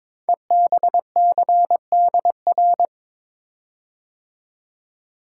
Morse code, heard as a keyed tone, about 22 wpm, 700 Hz. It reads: EBCDR
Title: Code Groups - 5 Letters - 22wpm